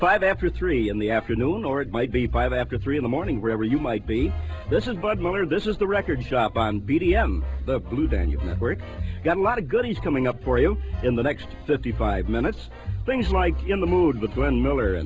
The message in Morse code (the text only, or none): none